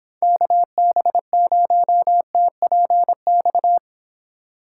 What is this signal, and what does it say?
Morse code, heard as a keyed tone, about 31 wpm, 700 Hz. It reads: KB0TPX